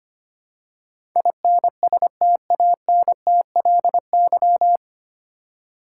Morse code, heard as a keyed tone, about 25 wpm, 700 Hz. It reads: INSTANTLY
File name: practice / 25wpm / 9 Letter Words - First 3 Letters Given - 25wpm